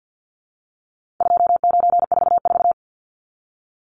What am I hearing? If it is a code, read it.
Morse code, heard as a keyed tone, about 50 wpm, 700 Hz. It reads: 2944
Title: Numbers - 4-Digit 50wpm